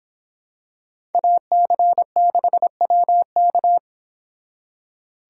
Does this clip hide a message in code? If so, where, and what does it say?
Yes: Morse code from 1.1 to 3.8 s, heard as a keyed tone, about 26 wpm, 700 Hz. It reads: AC6WK